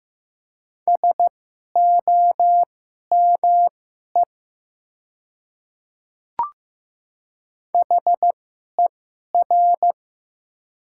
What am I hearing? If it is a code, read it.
Morse code, heard as a keyed tone, about 15 wpm, spaced as for 11 wpm, 700 Hz. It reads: SOME HER
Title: 2x Character Spacing - Top 100 Words - 15wpm